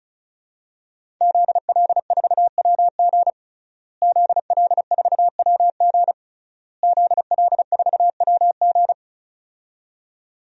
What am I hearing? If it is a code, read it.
Morse code, heard as a keyed tone, about 35 wpm, 700 Hz. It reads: ZL4WG ZL4WG ZL4WG
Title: DXCC-100 Call Signs - 5 Characters - Repeated 3x - 35wpm